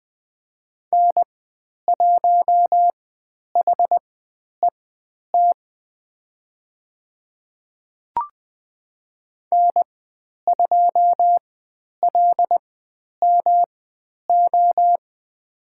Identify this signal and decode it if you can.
Morse code, heard as a keyed tone, about 20 wpm, spaced as for 10 wpm, 700 Hz. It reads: N1HET N2LMO
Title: 109 Lesson 18g - Letter m - Call Signs 20wpm